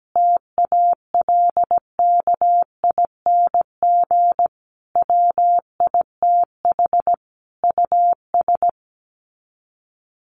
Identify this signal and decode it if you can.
Morse code, heard as a keyed tone, about 17 wpm, 700 Hz. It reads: TALKING WITH US
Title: Sets of 5 Words Encore 17wpm